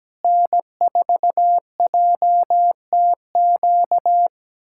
Morse code, heard as a keyed tone, about 17 wpm, 700 Hz. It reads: N4JTQ
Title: USA Callsign DE Callsign - 5 Characters - 17wpm